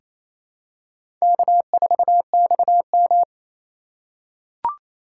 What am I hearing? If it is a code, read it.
Morse code, heard as a keyed tone, about 27 wpm, 700 Hz. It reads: K4XM
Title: USA Call Signs - 4x Character Spacing - 28wpm